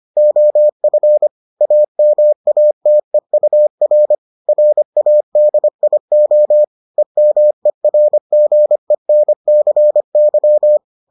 Morse code, heard as a keyed tone, about 25 wpm, 600 Hz. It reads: OF AMATEUR RADIO EMERGENCY